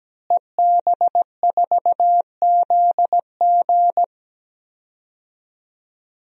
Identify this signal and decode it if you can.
Morse code, heard as a keyed tone, about 17 wpm, 700 Hz. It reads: EB4ZG
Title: DXCC-100 Call Signs - 5 Characters - Repeated 3x - 17wpm